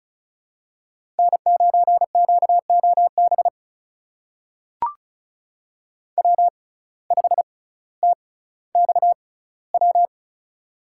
Morse code, heard as a keyed tone, about 36 wpm, spaced as for 18 wpm, 700 Hz. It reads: N9QOB W5TXW
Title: USA Call Signs - 5 Characters - 6x Character Spacing - 35wpm